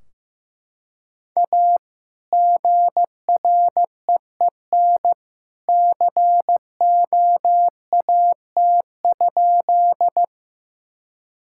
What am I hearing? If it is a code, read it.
Morse code, heard as a keyed tone, about 15 wpm, 700 Hz. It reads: A GREEN COAT?
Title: Sentences from Top 1600 Words 15wpm